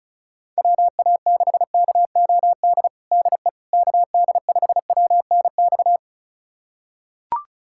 Morse code, heard as a keyed tone, about 35 wpm, 700 Hz. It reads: WA6KOD DE KD5WNX
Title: USA Callsign DE Callsign - 6 Characters - 35wpm